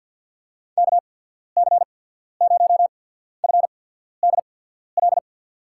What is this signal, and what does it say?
Morse code, heard as a keyed tone, about 50 wpm, 700 Hz. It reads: K C 0 F D L
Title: USA Call Signs - 6 Characters - 8x Character Spacing - 50wpm